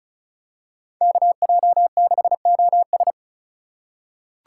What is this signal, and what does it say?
Morse code, heard as a keyed tone, about 36 wpm, 700 Hz. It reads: KJ6OS
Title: USA Call Signs - 5 Characters - 4x Character Spacing - 35wpm